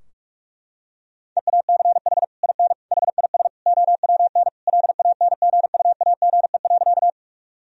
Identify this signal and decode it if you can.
Morse code, heard as a keyed tone, about 41 wpm, 700 Hz. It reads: EACH IN HIS OWN LANGUAGE.
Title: Sentences from Top 1000 Words - Review Part 3 - 45wpm